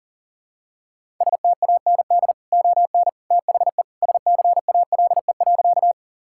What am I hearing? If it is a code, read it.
Morse code, heard as a keyed tone, about 40 wpm, 700 Hz. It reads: STAND ON THE SCALE.
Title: Sentences from Top 1000 Words - Review Part 5 - 40wpm